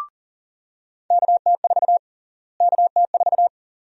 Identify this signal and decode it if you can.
Morse code, heard as a keyed tone, about 40 wpm, 700 Hz. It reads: KT4 KT4
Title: USA Call Sign Prefixes - Repeated 2x - 40wpm